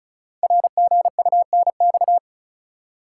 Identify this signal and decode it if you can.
Morse code, heard as a keyed tone, about 36 wpm, 690 Hz. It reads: RGUNX
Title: Code Groups - 5 Letters - 35wpm